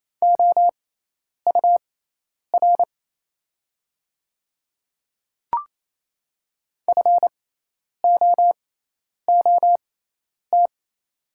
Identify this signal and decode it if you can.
Morse code, heard as a keyed tone, about 27 wpm, spaced as for 10 wpm, 690 Hz. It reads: OUR FOOT
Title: 6x Character Spacing - Top 500 Words - 28wpm